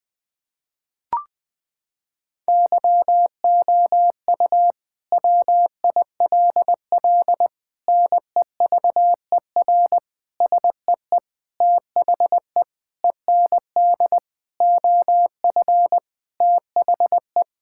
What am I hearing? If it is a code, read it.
Morse code, heard as a keyed tone, about 20 wpm, 700 Hz. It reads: YOU WILL NEVER SEE THE END OF THE
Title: Sentences from Top 1600 Words 20wpm